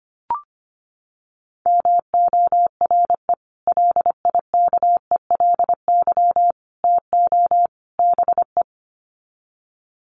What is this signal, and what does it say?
Morse code, heard as a keyed tone, about 25 wpm, 700 Hz. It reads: MORE LIKELY TO BE
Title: Sets of 4 Words 25wpm